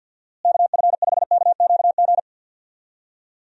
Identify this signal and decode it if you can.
Morse code, heard as a keyed tone, about 50 wpm, 700 Hz. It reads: KF5KQG
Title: USA Call Signs - 6 Characters - Repeated 2x - 50wpm